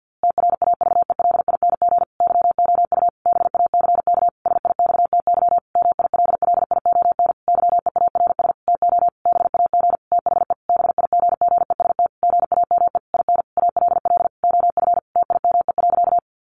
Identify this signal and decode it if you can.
Morse code, heard as a keyed tone, about 50 wpm, 700 Hz. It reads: TRAVELING YOU BACK SIXTY MILLION YEARS TO BAG THE BIGGEST GAME IN ALL OF TIME.
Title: A Sound Of Thunder 50wpm